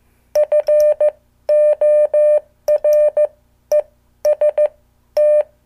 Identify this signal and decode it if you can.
Morse code, heard as a keyed tone, about 15 wpm, spaced as for 12 wpm, 600 Hz. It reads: FOREST
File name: morse / qso